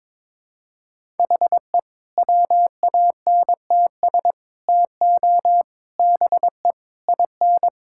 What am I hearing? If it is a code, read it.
Morse code, heard as a keyed tone, about 22 wpm, 700 Hz. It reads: HE WANTS TO BE IN
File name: practice / 22wpm / Sentences with 4-Word Phrases - 22wpm